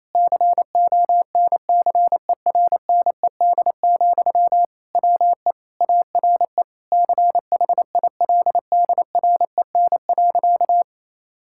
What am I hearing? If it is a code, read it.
Morse code, heard as a keyed tone, about 28 wpm, 700 Hz. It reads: CONCERNED, WE ARE CHILDREN.
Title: Sentences from Top 1000 Words - Review Part 6 - 28wpm